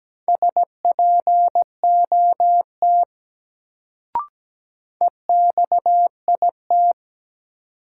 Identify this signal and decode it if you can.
Morse code, heard as a keyed tone, about 17 wpm, 700 Hz. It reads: SPOT EXIT
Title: Sentences from Top 1100 Words - New Word Drill - 17wpm